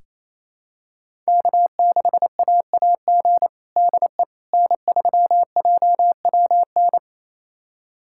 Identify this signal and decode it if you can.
Morse code, heard as a keyed tone, about 28 wpm, 700 Hz. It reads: K6AAG DE N3JWN
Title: USA Callsign DE Callsign - 5 Characters - 28wpm